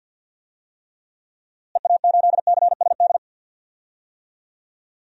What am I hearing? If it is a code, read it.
Morse code, heard as a keyed tone, about 46 wpm, 700 Hz. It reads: EA8CSD
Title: DXCC-100 Call Signs - 6 Characters - Repeated 3x - 50wpm